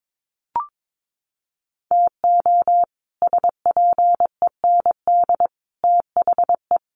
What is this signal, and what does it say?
Morse code, heard as a keyed tone, about 22 wpm, 700 Hz. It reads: TO SPEND THE
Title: Sets of 5 Words Encore2 22wpm